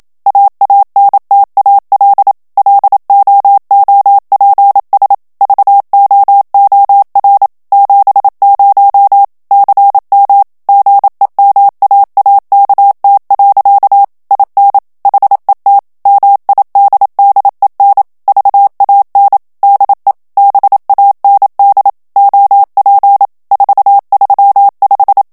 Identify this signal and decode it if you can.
Morse code, heard as a keyed tone, about 27 wpm, 800 Hz. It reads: AANTAL LOOPS VOOR 70 CM GEMAAKT. IN HET MIDDEN VAN DE BAND OP 435
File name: HOLLANDAIS texte clair 1 vitesse 26 mots minute